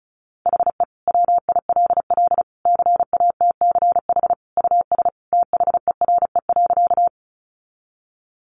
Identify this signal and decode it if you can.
Morse code, heard as a keyed tone, about 34 wpm, 700 Hz. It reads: HE WILL CATCH US THERE.
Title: Sentences from Top 1500 Words 35wpm